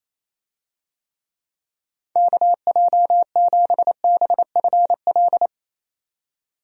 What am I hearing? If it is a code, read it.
Morse code, heard as a keyed tone, about 27 wpm, 700 Hz. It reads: KJ7BFL
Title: USA Call Signs - 6 Characters - 28wpm